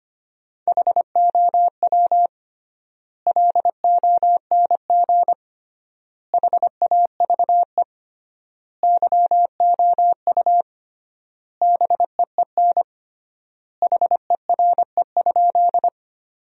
Morse code, heard as a keyed tone, about 25 wpm, 700 Hz. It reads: HOW LONG HAVE YOU BEEN HERE?
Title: 3x Word-Spacing - Sentences from Top 200 Words - 25wpm